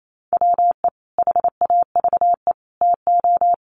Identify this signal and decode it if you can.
Morse code, heard as a keyed tone, about 28 wpm, 700 Hz. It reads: WE HAVE TO
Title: Sets of 5 Words Encore2 28wpm